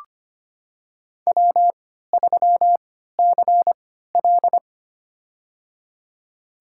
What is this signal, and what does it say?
Morse code, heard as a keyed tone, about 25 wpm, spaced as for 14 wpm, 700 Hz. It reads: W3CL
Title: USA Call Signs - 4 Characters - 3x Character Spacing - 25wpm